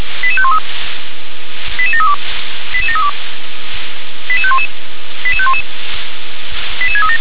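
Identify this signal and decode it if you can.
Five-tone selective call (ZVEI), heard as quick runs of five tones, tones of about 70 ms. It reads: 88513 88732 88732 88511 88511 88637
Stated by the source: ZVEI five-tone call 88501 received by SDR.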